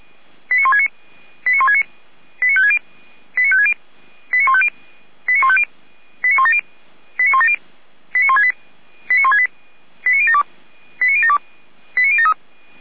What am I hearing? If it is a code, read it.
Five-tone selective call (ZVEI), heard as quick runs of five tones, tones of about 70 ms. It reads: 89169 89169 89560 89560 89150 89150 89170 89170 89168 89168 89072 89072 89073
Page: ZVEI